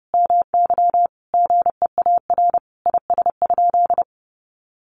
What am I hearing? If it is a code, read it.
Morse code, heard as a keyed tone, about 30 wpm, 700 Hz. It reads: MY GEAR IS?
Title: Sentences from Top 1400 Words 30wpm